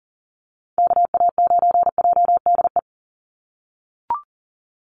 Morse code, heard as a keyed tone, about 38 wpm, 700 Hz. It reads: KA9JDE
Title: USA Call Signs 40wpm